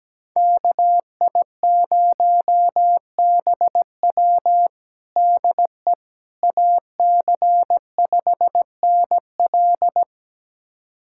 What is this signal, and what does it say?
Morse code, heard as a keyed tone, about 17 wpm, 700 Hz. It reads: KI0BW DE AC5NL
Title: USA Callsign DE Callsign - 5 Characters - 17wpm